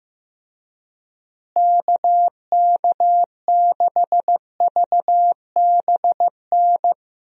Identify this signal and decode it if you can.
Morse code, heard as a keyed tone, about 15 wpm, 700 Hz. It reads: KK6VBN
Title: USA Call Signs - 6 Characters - Repeated 3x - 15wpm